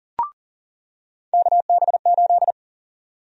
Morse code, heard as a keyed tone, about 39 wpm, 700 Hz. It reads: KB8
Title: USA Call Sign Prefixes - 40wpm